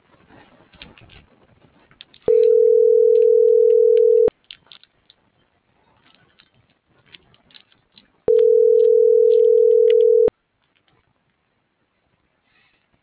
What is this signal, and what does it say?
Ringback tone (North America), 440 + 480 Hz.